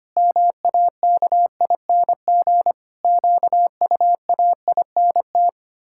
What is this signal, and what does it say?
Morse code, heard as a keyed tone, about 25 wpm, 700 Hz. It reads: MAKING QUAINT
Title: Pangram Challenge v2 - 25wpm